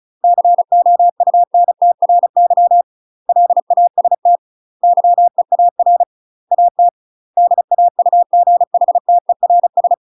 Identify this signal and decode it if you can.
Morse code, heard as a keyed tone, about 34 wpm, 700 Hz. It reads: COUNTRY LAST YEAR AT DAUGHTERS
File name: UK MASTER QSOs TRAINING / 35WPM / UK MASTER QSOs 0034